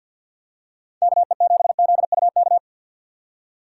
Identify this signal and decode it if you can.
Morse code, heard as a keyed tone, about 51 wpm, 700 Hz. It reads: KE7ZUK
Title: USA Call Signs - 6 Characters - 3x Character Spacing - 50wpm